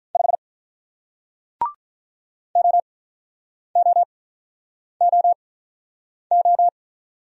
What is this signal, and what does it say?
Morse code, heard as a keyed tone, about 48 wpm, spaced as for 9 wpm, 700 Hz. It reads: 5 OOOO